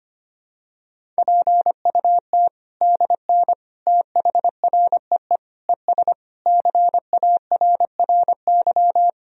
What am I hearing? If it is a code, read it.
Morse code, heard as a keyed tone, about 25 wpm, 700 Hz. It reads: PUT DN THREE ES CARRY
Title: Sentences from Top 400 Words and Abbreviations 25wpm